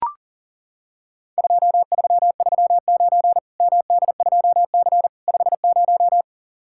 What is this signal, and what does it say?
Morse code, heard as a keyed tone, about 40 wpm, 700 Hz. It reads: 2339 MD2C 50
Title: Worked All Europe DX Contest - QTC Exchange - 40wpm